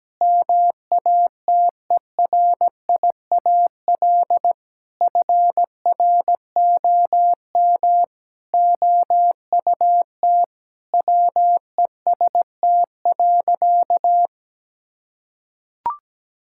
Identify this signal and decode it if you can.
Morse code, heard as a keyed tone, about 17 wpm, 700 Hz. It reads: MATERIAL FROM OUT WEST.
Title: Sentences from Top 1000 Words - Review Part 5 - 17wpm